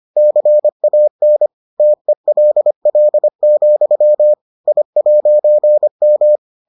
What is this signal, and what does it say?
Morse code, heard as a keyed tone, about 25 wpm, 600 Hz. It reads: CAN TELL, I'M